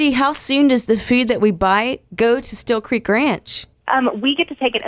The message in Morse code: none